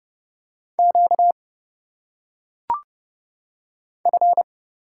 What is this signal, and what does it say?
Morse code, heard as a keyed tone, about 29 wpm, 700 Hz. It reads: Q F
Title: Single Letters - Phonetic Pronunciation 30wpm